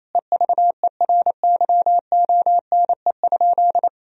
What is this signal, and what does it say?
Morse code, heard as a keyed tone, about 28 wpm, 700 Hz. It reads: EVERYONE?
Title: Sentences from Top 1000 Words - Review Part 4 - 28wpm